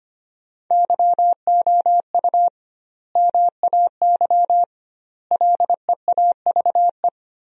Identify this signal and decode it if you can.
Morse code, heard as a keyed tone, about 25 wpm, 700 Hz. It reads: YOU MAY LEAVE
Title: 2x Word-Spacing - Sentences from Top 400 Words - 25wpm